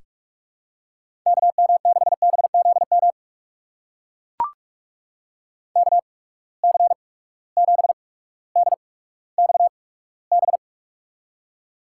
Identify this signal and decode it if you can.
Morse code, heard as a keyed tone, about 44 wpm, spaced as for 19 wpm, 700 Hz. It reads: KM6BZM KC7DXB
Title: USA Call Signs - 6 Characters - 8x Character Spacing - 45wpm